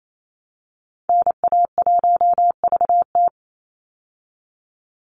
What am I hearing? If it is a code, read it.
Morse code, heard as a keyed tone, about 28 wpm, 700 Hz. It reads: NA1VT
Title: Call Signs Repeated 2x - 28wpm